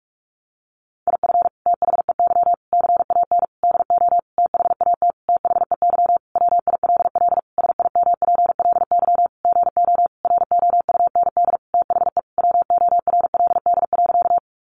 Morse code, heard as a keyed tone, about 44 wpm, 700 Hz. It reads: IF THEY CAN DO THAT THEY WILL SIMPLY GO ROUND THE WORLD.
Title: Sentences from Top 600 Words 45wpm